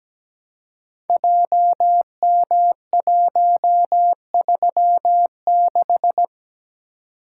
Morse code, heard as a keyed tone, about 17 wpm, 700 Hz. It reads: JM136
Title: Code Groups - 5 Letter:Numbers - 17wpm